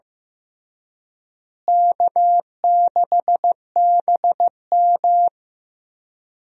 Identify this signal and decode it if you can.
Morse code, heard as a keyed tone, about 15 wpm, 700 Hz. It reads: K6BM
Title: USA Call Signs - 4 Characters - Repeated 2x - 15wpm